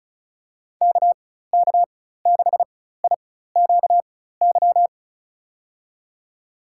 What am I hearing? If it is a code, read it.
Morse code, heard as a keyed tone, about 36 wpm, spaced as for 16 wpm, 700 Hz. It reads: KK6IQY
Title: USA Call Signs - 6 Characters - 4x Character Spacing - 35wpm